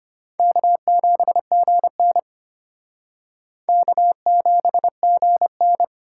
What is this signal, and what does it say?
Morse code, heard as a keyed tone, about 34 wpm, 700 Hz. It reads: K7GN K7GN